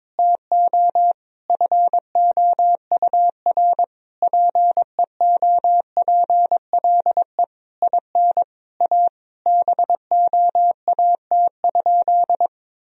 Morse code, heard as a keyed tone, about 21 wpm, 700 Hz. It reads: TO FOUR PEOPLE IN A BOAT?
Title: Sentences from Top 1000 Words - Review Part 3 - 22wpm